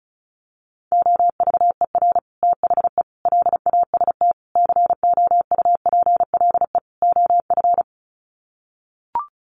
Morse code, heard as a keyed tone, about 34 wpm, 700 Hz. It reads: OVER THE LAST COUPLE OF
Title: Sets of 5 Words Encore 35wpm